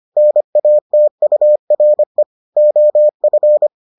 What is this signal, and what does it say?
Morse code, heard as a keyed tone, about 25 wpm, 600 Hz. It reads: NATURE OF